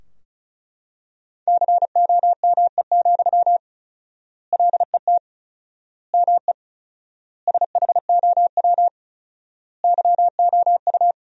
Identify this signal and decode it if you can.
Morse code, heard as a keyed tone, about 36 wpm, 700 Hz. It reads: COME, LET ME SHOW YOU
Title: 4x Word-Spacing - Sentences from Top 500 Words - 35wpm